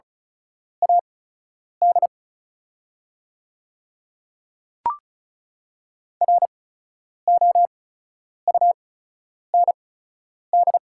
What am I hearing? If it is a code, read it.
Morse code, heard as a keyed tone, about 36 wpm, spaced as for 10 wpm, 690 Hz. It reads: AD ROUND
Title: 8x Character Spacing - Top 200 Words - 35wpm